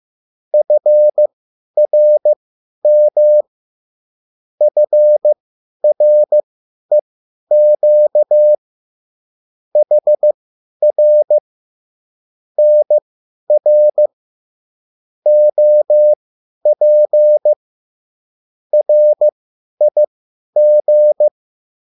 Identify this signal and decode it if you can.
Morse code, heard as a keyed tone, about 15 wpm, spaced as for 10 wpm, 600 Hz. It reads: FRM FREQ HR NR OP RIG